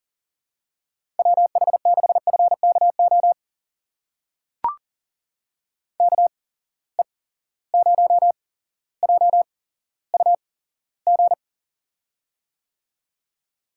Morse code, heard as a keyed tone, about 40 wpm, spaced as for 17 wpm, 700 Hz. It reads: WH6FKO KE0JUG
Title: USA Call Signs - 6 Characters - 8x Character Spacing - 40wpm